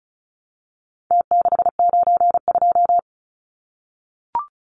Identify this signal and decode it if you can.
Morse code, heard as a keyed tone, about 34 wpm, 700 Hz. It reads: T692
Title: Numbers - Contest Style - 4-Digit 35wpm